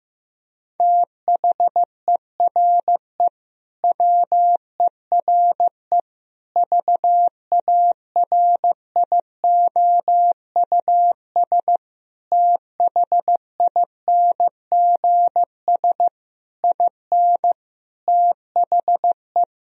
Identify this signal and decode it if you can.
Morse code, heard as a keyed tone, about 15 wpm, 700 Hz. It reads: THERE WERE VARIOUS THINGS IN THE
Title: Sentences from Top 1000 Words - Review Part 4 - 15wpm